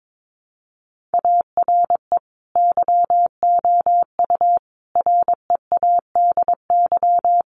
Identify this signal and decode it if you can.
Morse code, heard as a keyed tone, about 22 wpm, 700 Hz. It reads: ARE YOU READY
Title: Sentences from Top 1500 Words 22wpm